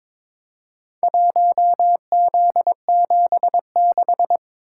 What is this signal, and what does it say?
Morse code, heard as a keyed tone, about 22 wpm, 700 Hz. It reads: 1Z76
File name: practice / 22wpm / Four Letter-Number Combinations - 22wpm